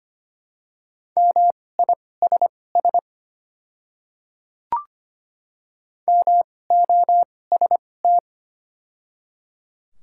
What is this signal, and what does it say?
Morse code, heard as a keyed tone, about 25 wpm, spaced as for 18 wpm, 700 Hz. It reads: MISS MOST
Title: 2x Character Spacing - Top 500 Words - 25wpm